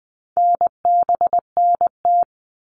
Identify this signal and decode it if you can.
Morse code, heard as a keyed tone, about 20 wpm, 700 Hz. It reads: NBNT
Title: All Cut Numbers - 4-Digit 20wpm